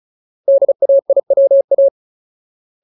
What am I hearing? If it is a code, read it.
Morse code, heard as a keyed tone, about 34 wpm, 550 Hz. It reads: DAIWA
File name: Ultimate Word List / 35WPM Ultimate